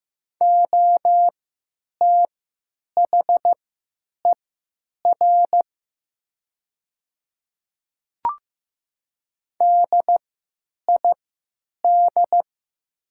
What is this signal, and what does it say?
Morse code, heard as a keyed tone, about 16 wpm, spaced as for 9 wpm, 700 Hz. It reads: OTHER DID
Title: 3x Character Spacing - Top 200 Words - 15wpm